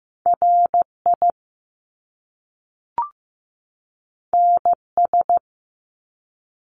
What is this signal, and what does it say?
Morse code, heard as a keyed tone, about 15 wpm, 700 Hz. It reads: RI NS
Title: Top Two Letter Combinations in Words 15wpm